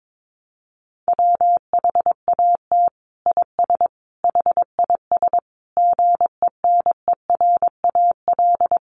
Morse code, heard as a keyed tone, about 22 wpm, 700 Hz. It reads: WHAT IS HIS GENERAL
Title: Sentences from Top 1000 Words 22wpm